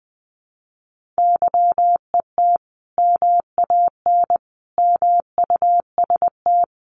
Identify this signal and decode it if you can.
Morse code, heard as a keyed tone, about 20 wpm, 700 Hz. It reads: YET MAN MUST
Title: Sentences from Top 1500 Words 20wpm